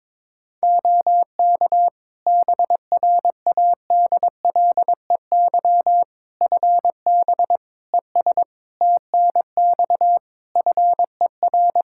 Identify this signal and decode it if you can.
Morse code, heard as a keyed tone, about 21 wpm, 700 Hz. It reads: OK BRADLEY FB ES TNX FER
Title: Practice QSO - Exchange 2 - Info - 22wpm